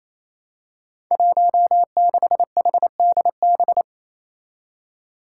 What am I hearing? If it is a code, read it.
Morse code, heard as a keyed tone, about 29 wpm, 700 Hz. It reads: 16HDB
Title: Five Challenging Letter-Number Combinations 28wpm